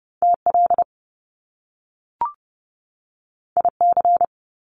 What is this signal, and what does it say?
Morse code, heard as a keyed tone, about 30 wpm, 700 Hz. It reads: TL IC
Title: Top Two Letter Combinations in Words 30wpm